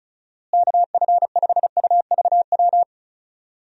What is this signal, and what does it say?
Morse code, heard as a keyed tone, about 34 wpm, 700 Hz. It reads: KF5UVW
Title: USA Call Signs - 6 Characters - Repeated 2x - 35wpm